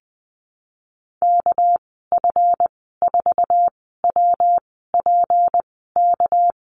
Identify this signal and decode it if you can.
Morse code, heard as a keyed tone, about 20 wpm, spaced as for 15 wpm, 700 Hz. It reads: KF4WPK
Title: 200 Lesson 32g - Letter k - Call Signs 20wpm